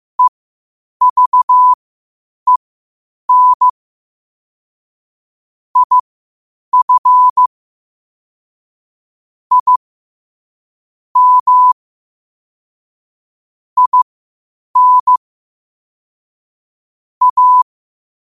Morse code, heard as a keyed tone, about 15 wpm, spaced as for 8 wpm, 1000 Hz. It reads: EVEN IF I M IN A
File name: qotdfiles / 8wpm